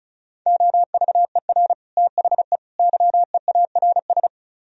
Morse code, heard as a keyed tone, about 44 wpm, 700 Hz. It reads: OVER THE YEARS
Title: Speed-Racing - Sets of 3 Words 35wpm